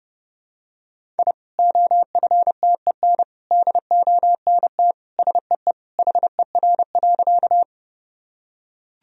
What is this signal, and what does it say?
Morse code, heard as a keyed tone, about 30 wpm, 700 Hz. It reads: I OFTEN DONT SEE HER.
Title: Sentences from Top 1000 Words - Review Part 1 - 30wpm